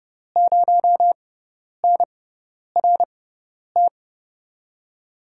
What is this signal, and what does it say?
Morse code, heard as a keyed tone, about 32 wpm, spaced as for 10 wpm, 700 Hz. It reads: 0NRT
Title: USA Call Signs - 6x Character Spacing - 30wpm